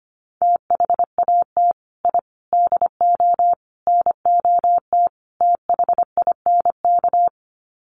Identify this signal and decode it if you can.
Morse code, heard as a keyed tone, about 25 wpm, 700 Hz. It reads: THAT I DO NOT THINK
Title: Sets of 5 Words Encore 25wpm